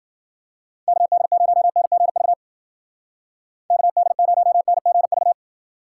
Morse code, heard as a keyed tone, about 50 wpm, 700 Hz. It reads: KD0NGV KD0NGV
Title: Speed Racing - USA 2x3 Call Signs 40wpm